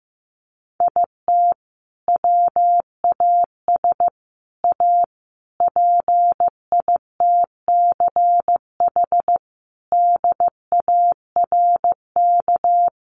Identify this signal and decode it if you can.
Morse code, heard as a keyed tone, about 15 wpm, 700 Hz. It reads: IT WAS A PITCH DARK
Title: Sentences from Top 1300 Words 15wpm